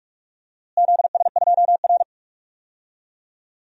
Morse code, heard as a keyed tone, about 41 wpm, 700 Hz. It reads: ZS2R
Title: DXCC-100 Call Signs - 4 Characters - Repeated 3x - 45wpm